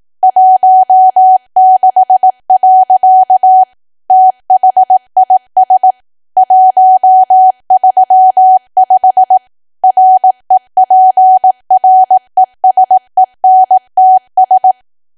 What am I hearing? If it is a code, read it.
Morse code, heard as a keyed tone, about 18 wpm, 750 Hz. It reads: 16. THIS 135 REPRESENTS
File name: cwa / W1AW-Level2 / 18WPM